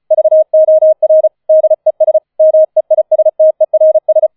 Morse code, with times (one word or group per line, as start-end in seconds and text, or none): 0.1-1.3 VOR
1.5-2.2 DES
2.4-4.3 MEISTERS